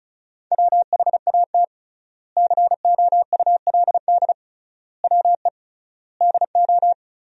Morse code, heard as a keyed tone, about 36 wpm, 700 Hz. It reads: WHAT COULD WE DO
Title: 3x Word-Spacing - Sentences from Top 400 Words - 35wpm